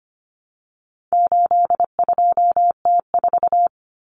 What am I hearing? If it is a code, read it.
Morse code, heard as a keyed tone, about 25 wpm, 700 Hz. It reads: 82T4
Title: Common Cut Numbers - 4-Digit 25wpm